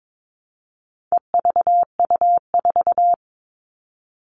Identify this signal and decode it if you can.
Morse code, heard as a keyed tone, about 22 wpm, 700 Hz. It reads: EVU4
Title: All Cut Numbers - 4-Digit 22wpm